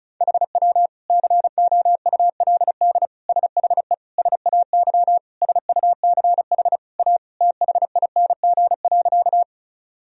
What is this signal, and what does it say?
Morse code, heard as a keyed tone, about 34 wpm, 700 Hz. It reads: HW COULD SHE SAY SUCH A THING.
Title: Sentences from Top 200 Words and Abbreviations 35wpm